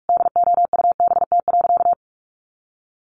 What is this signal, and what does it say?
Morse code, heard as a keyed tone, about 44 wpm, 700 Hz. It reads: DOUBT.
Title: Sentences from Top 1500 Words 45wpm